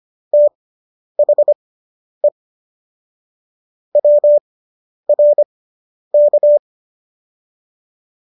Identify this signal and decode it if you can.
Morse code, heard as a keyed tone, about 25 wpm, spaced as for 10 wpm, 590 Hz. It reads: THE WRK